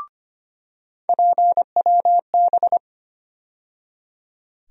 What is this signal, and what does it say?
Morse code, heard as a keyed tone, about 25 wpm, 700 Hz. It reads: PWB
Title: USA Call Sign Postfixes - 3 Letters 25wpm